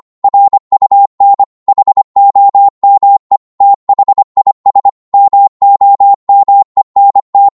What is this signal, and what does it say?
Morse code, heard as a keyed tone, about 25 wpm, 800 Hz. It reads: RUN HOME THIS MOMENT
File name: 25wpm